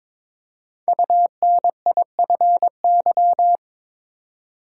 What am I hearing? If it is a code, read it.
Morse code, heard as a keyed tone, about 22 wpm, 700 Hz. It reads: UNIFY
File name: 5 Letter Words - First 3 Letters Given - 22wpm